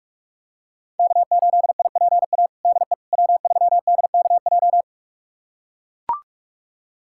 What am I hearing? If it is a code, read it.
Morse code, heard as a keyed tone, about 41 wpm, 700 Hz. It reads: K8IPA DE W3DKJ
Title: USA Callsign DE Callsign - 5 Characters - 45wpm